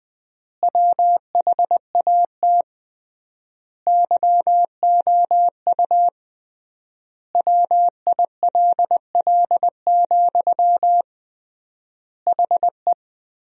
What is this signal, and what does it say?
Morse code, heard as a keyed tone, about 21 wpm, 700 Hz. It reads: WHAT YOU WILL, HE
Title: 3x Word-Spacing - Sentences from Top 200 Words - 20wpm